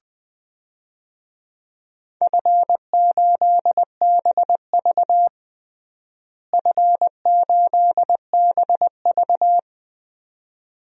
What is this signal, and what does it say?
Morse code, heard as a keyed tone, about 20 wpm, 700 Hz. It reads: F8BV F8BV
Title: DXCC-100 Call Signs - 4 Characters - Repeated 2x - 20wpm